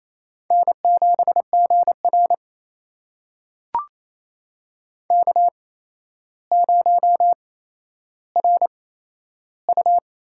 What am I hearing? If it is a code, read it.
Morse code, heard as a keyed tone, about 28 wpm, spaced as for 12 wpm, 700 Hz. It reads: N7GR K0RU
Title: USA Call Signs - 4 Characters - 8x Character Spacing - 28wpm